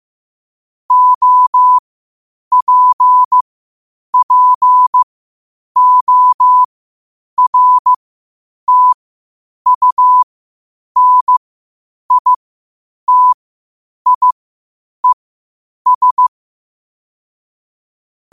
Morse code, heard as a keyed tone, about 15 wpm, spaced as for 8 wpm, 1000 Hz. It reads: OPPORTUNITIES
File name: qotdfiles / 8wpm